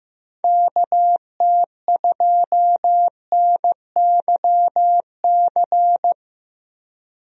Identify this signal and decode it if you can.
Morse code, heard as a keyed tone, about 15 wpm, 700 Hz. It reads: KT2NYC
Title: USA Call Signs - 6 Characters - 15wpm